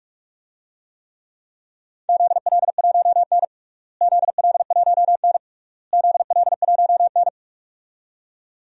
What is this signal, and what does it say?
Morse code, heard as a keyed tone, about 46 wpm, 700 Hz. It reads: ZL1N ZL1N ZL1N
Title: DXCC-100 Call Signs - 4 Characters - Repeated 3x - 45wpm